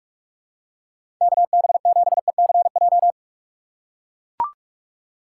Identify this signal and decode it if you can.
Morse code, heard as a keyed tone, about 44 wpm, 700 Hz. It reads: KB7ECJ
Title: USA Call Signs - 6 Characters - 6x Character Spacing - 45wpm